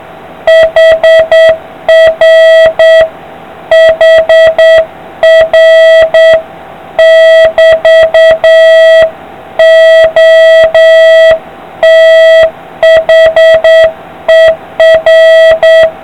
Morse code, heard as a keyed tone, about 8 wpm, 650 Hz, with noise. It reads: HR HR<BT>OTHER